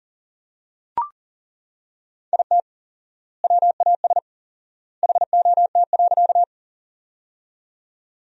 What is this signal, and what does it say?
Morse code, heard as a keyed tone, about 40 wpm, 700 Hz. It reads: IT WAS HOT.
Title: 4x Word-Spacing - Sentences from Top 500 Words - 40wpm